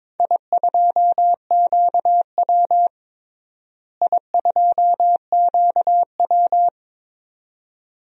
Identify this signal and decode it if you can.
Morse code, heard as a keyed tone, about 22 wpm, 700 Hz. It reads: I2QW I2QW
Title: DXCC-100 Call Signs - 4 Characters - Repeated 2x - 22wpm